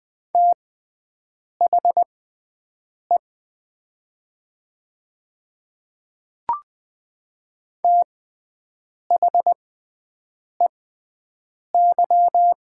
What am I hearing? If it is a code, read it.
Morse code, heard as a keyed tone, about 20 wpm, spaced as for 7 wpm, 700 Hz. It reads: THE THEY
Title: 6x Character Spacing - Top 100 Words - 20wpm